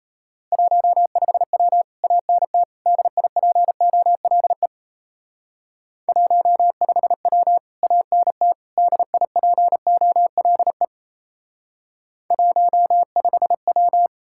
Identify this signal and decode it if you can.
Morse code, heard as a keyed tone, about 34 wpm, 700 Hz. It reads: 15W ANT DIPOLE 15W ANT DIPOLE 15W
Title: Speed-Racing Common QSO Phrases - 28wpm